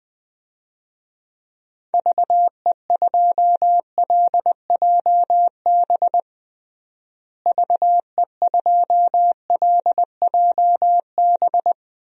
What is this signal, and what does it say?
Morse code, heard as a keyed tone, about 21 wpm, 700 Hz. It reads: VE2LJB VE2LJB
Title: DXCC-100 Call Signs - 6 Characters - Repeated 2x - 20wpm